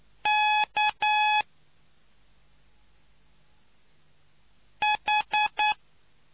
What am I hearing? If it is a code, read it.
Morse code, heard as a keyed tone, about 9 wpm, spaced as for 2 wpm, 850 Hz. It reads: KH